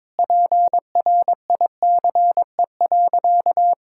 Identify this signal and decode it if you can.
Morse code, heard as a keyed tone, about 22 wpm, 700 Hz. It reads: PRICE.
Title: Sentences from Top 1000 Words - Review Part 6 - 22wpm